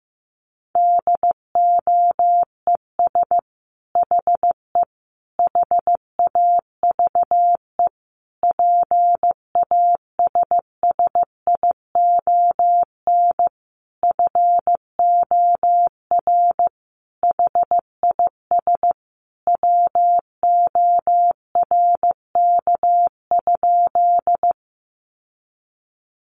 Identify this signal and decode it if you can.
Morse code, heard as a keyed tone, about 15 wpm, 700 Hz. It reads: DOES HE HAVE PASSION FOR HIS WORK?
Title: Sentences from Top 1300 Words 15wpm